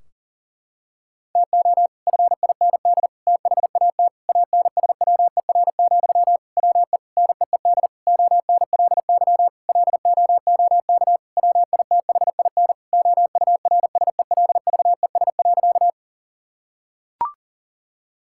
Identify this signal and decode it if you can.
Morse code, heard as a keyed tone, about 40 wpm, 700 Hz. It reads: TO FIND THAT ANSWER, WE NEED ONLY LOOK WITHIN OURSELVES.
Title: Sentences from Top 1600 Words 40wpm